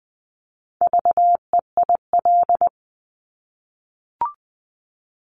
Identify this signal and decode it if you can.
Morse code, heard as a keyed tone, about 20 wpm, 700 Hz. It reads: VEIL
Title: Multi-letter Phonemes Repeated 3x 20wpm